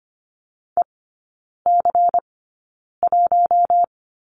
Morse code, heard as a keyed tone, about 25 wpm, spaced as for 9 wpm, 700 Hz. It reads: EC1
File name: Lesson 17e - Letter u - Three Characters 25wpm